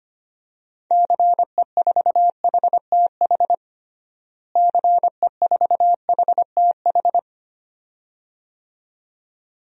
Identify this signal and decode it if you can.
Morse code, heard as a keyed tone, about 25 wpm, 700 Hz. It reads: CE4HTH CE4HTH
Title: DXCC-100 Call Signs - 6 Characters - Repeated 2x - 25wpm